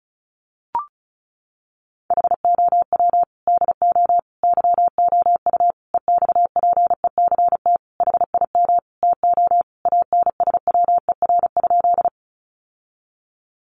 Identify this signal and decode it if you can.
Morse code, heard as a keyed tone, about 34 wpm, 700 Hz. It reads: HOW DO YOU EXPECT HIM TO ANSWER?
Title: Sentences from Top 1100 Words 35wpm